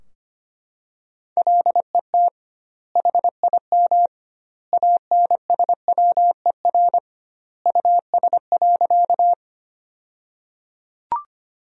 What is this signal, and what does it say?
Morse code, heard as a keyed tone, about 25 wpm, 700 Hz. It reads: LET HIM ANSWER US.
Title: 2x Word-Spacing - Sentences from Top 300 Words - 25wpm